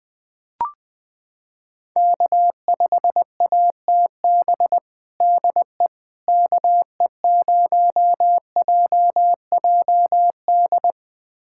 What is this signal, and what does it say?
Morse code, heard as a keyed tone, about 20 wpm, 700 Hz. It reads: K5ATB DE KE0JJD
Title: USA Callsign DE Callsign 20wpm